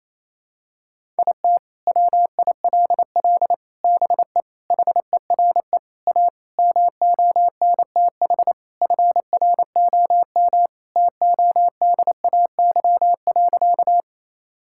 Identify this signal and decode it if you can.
Morse code, heard as a keyed tone, about 28 wpm, 700 Hz. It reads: IT WILL BE HERE A MONTH FROM TODAY.
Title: Sentences from Top 1000 Words - Review Part 5 - 28wpm